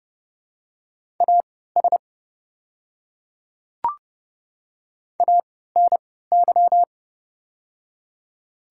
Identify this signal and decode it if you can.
Morse code, heard as a keyed tone, about 30 wpm, spaced as for 17 wpm, 700 Hz. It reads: AS ANY
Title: 3x Character Spacing - Top 200 Words - 30wpm